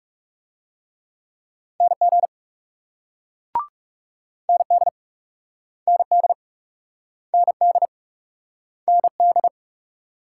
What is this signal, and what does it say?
Morse code, heard as a keyed tone, about 38 wpm, 700 Hz. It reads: NG ND ND ND ND